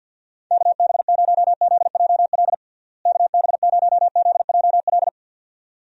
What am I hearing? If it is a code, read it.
Morse code, heard as a keyed tone, about 48 wpm, 700 Hz. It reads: KB0ZJL KB0ZJL
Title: USA Call Signs - 6 Characters - Repeated 2x - 50wpm